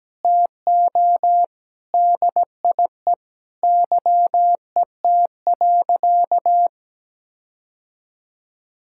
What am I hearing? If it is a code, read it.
Morse code, heard as a keyed tone, about 17 wpm, 700 Hz. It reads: TO DIE YET.
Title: Sentences from Top 1600 Words 17wpm